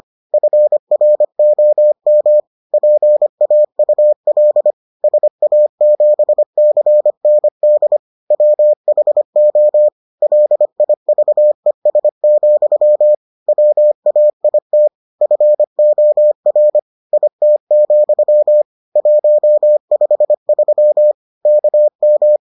Morse code, heard as a keyed tone, about 25 wpm, 600 Hz. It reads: FROM PAUL SA7CND WHO LIVES, WAIT FOR IT, 153 KM